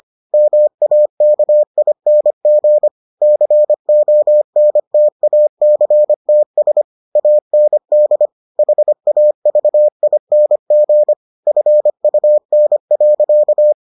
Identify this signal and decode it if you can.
Morse code, heard as a keyed tone, about 25 wpm, 600 Hz. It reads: MAKING CONTACTS AND HAVING FUN.